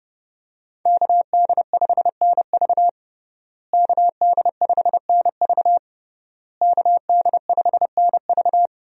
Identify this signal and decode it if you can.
Morse code, heard as a keyed tone, about 30 wpm, 700 Hz. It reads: KD5NV KD5NV KD5NV
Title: USA Call Signs - 5 Characters - Repeated 3x - 30wpm